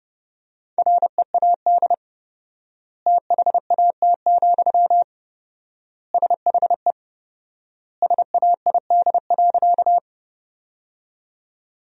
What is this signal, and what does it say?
Morse code, heard as a keyed tone, about 32 wpm, 700 Hz. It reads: READ THAT, SHE SAID.
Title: 4x Word-Spacing - Sentences from Top 200 Words - 30wpm